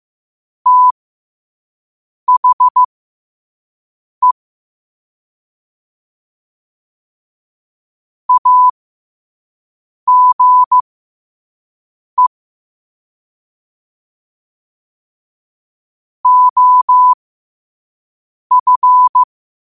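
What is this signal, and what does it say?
Morse code, heard as a keyed tone, about 15 wpm, spaced as for 5 wpm, 1000 Hz. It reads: THE AGE OF
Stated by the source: Morse code quote of the day at 5 wpm.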